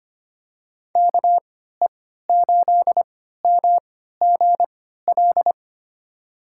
Morse code, heard as a keyed tone, about 25 wpm, spaced as for 14 wpm, 700 Hz. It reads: KE8MGL
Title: USA Call Signs - 3x Character Spacing - 25wpm